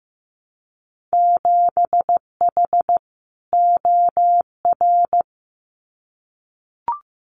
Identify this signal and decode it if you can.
Morse code, heard as a keyed tone, about 15 wpm, 700 Hz. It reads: 7H OR
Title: Winter Field Day Exchanges Encore 15wpm